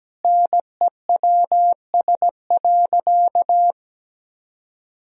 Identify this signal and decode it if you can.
Morse code, heard as a keyed tone, about 17 wpm, 700 Hz. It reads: NEWS.
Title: Sentences from Top 1000 Words - Review Part 2 - 17wpm